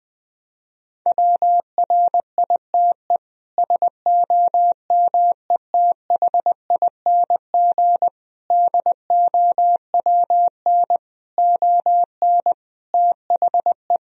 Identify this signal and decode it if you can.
Morse code, heard as a keyed tone, about 21 wpm, 700 Hz. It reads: WRITE SOMETHING DOWN ON THE
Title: Sentences from Top 1000 Words - Review Part 1 - 20wpm